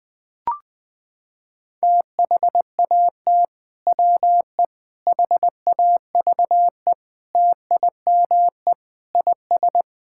Morse code, heard as a keyed tone, about 21 wpm, 700 Hz. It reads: THAT WE HAVE TIME IS
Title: Sentences with 3-Word Phrases - 20wpm